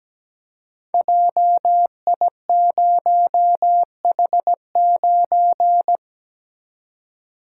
Morse code, heard as a keyed tone, about 17 wpm, 700 Hz. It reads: JI0H9
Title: Code Groups - 5 Letter:Numbers - 17wpm